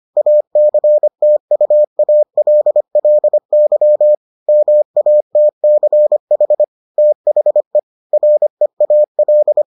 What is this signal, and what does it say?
Morse code, heard as a keyed tone, about 25 wpm, 600 Hz. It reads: ACTUALLY MATCH THE REAL